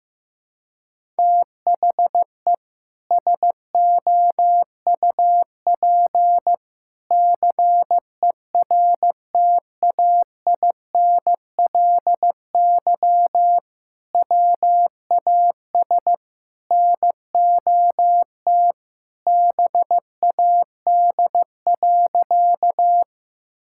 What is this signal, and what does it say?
Morse code, heard as a keyed tone, about 15 wpm, 700 Hz. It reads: THE SOUP CERTAINLY WAS NOT BAD.
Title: Sentences from Top 1000 Words - Review Part 6 - 15wpm